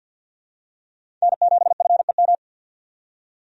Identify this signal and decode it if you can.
Morse code, heard as a keyed tone, about 51 wpm, 700 Hz. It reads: N7FEM
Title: USA Call Signs - 5 Characters - 50wpm